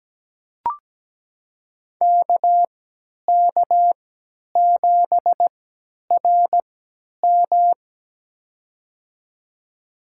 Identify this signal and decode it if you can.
Morse code, heard as a keyed tone, about 17 wpm, spaced as for 10 wpm, 700 Hz. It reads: KK7RM
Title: USA Call Signs - 5 Characters - 3x Character Spacing - 17wpm